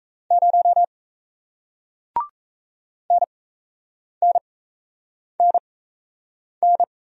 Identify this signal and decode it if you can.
Morse code, heard as a keyed tone, about 39 wpm, spaced as for 8 wpm, 700 Hz. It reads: 0 NNNN